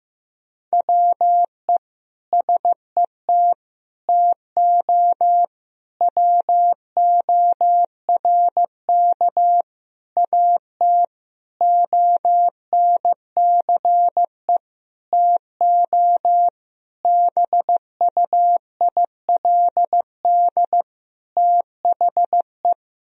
Morse code, heard as a keyed tone, about 16 wpm, 700 Hz. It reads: WE SET TO WORK AT ONCE TO BUILD THE
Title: Sentences from Top 1000 Words - Review Part 1 - 15wpm